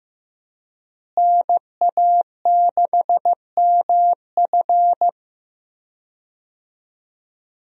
Morse code, heard as a keyed tone, about 16 wpm, 700 Hz. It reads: NA6MF
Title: USA Call Signs - 5 Characters - Repeated 2x - 15wpm